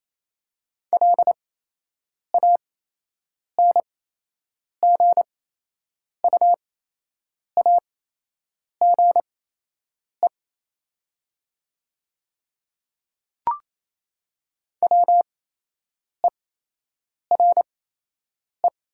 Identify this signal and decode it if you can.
Morse code, heard as a keyed tone, about 28 wpm, spaced as for 8 wpm, 700 Hz. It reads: LANGUAGE WERE
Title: 8x Character Spacing - Top 500 Words - 28wpm